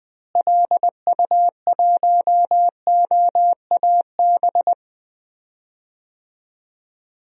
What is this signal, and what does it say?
Morse code, heard as a keyed tone, about 19 wpm, 700 Hz. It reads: LU1OAB
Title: DXCC-100 Call Signs - 6 Characters - Repeated 3x - 20wpm